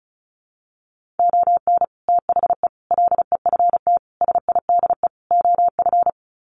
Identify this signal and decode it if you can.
Morse code, heard as a keyed tone, about 34 wpm, 700 Hz. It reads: ON THE LEFT SIDE OF
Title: Sets of 5 Words Encore2 35wpm